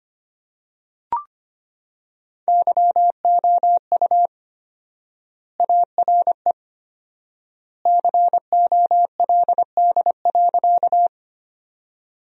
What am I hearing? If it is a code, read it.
Morse code, heard as a keyed tone, about 25 wpm, 700 Hz. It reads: YOU ARE COLD.
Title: 4x Word-Spacing - Sentences from Top 400 Words - 25wpm